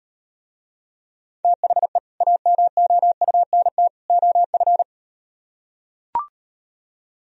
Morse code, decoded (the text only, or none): THE AMOUNT OF